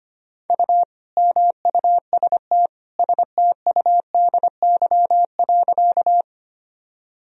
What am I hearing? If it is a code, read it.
Morse code, heard as a keyed tone, about 25 wpm, 700 Hz. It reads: U MUST STUDY.
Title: Sentences from Top 300 Words and Abbreviations 25wpm